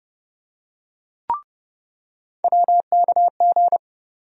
Morse code, heard as a keyed tone, about 31 wpm, 700 Hz. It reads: WKG
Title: USA Call Sign Postfixes - 3 Letters 30wpm